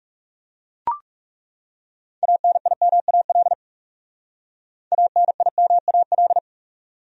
Morse code, heard as a keyed tone, about 39 wpm, 700 Hz. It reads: ANIMAL ANIMAL